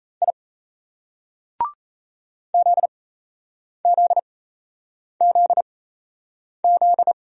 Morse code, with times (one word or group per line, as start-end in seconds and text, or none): 0.2-0.3 I
2.5-7.1 ZZZZ